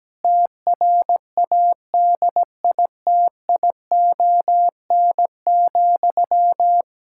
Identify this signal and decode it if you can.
Morse code, heard as a keyed tone, about 17 wpm, 700 Hz. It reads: TRADITION,
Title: Pangram Challenge v2 - 17wpm